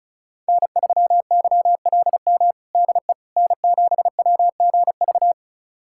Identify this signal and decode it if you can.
Morse code, heard as a keyed tone, about 34 wpm, 700 Hz. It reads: N3YLM DE N7WGV
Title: USA Callsign DE Callsign - 5 Characters - 35wpm